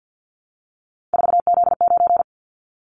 Morse code, heard as a keyed tone, about 50 wpm, 700 Hz. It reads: 4Z9
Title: Callsign Prefixes for Top 50 DXCC - 50wpm